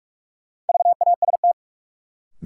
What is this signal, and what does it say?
Morse code, heard as a keyed tone, about 46 wpm, 700 Hz. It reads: VAST
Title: Sets of 4 Words - New Word Drill - 45wpm